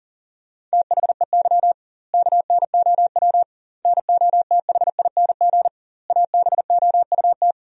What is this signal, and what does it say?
Morse code, heard as a keyed tone, about 40 wpm, 700 Hz. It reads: THEY KNOW NOTHING ABOUT
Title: 2x Word-Spacing - Sentences from Top 500 Words - 40wpm